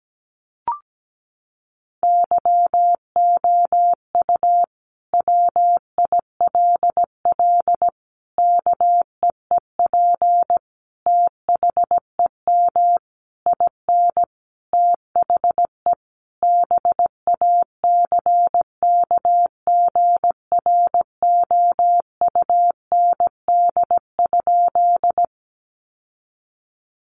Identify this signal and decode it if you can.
Morse code, heard as a keyed tone, about 17 wpm, 700 Hz. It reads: YOU WILL KEEP THEM IN THE BACKGROUND?
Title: Sentences from Top 1300 Words 17wpm